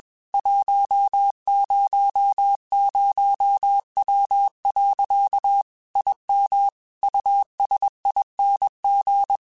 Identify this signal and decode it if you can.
Morse code, heard as a keyed tone, about 21 wpm, 800 Hz. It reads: 100W. IM USING
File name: QSOs / QSOs-mp3s / WD0FBT 1@20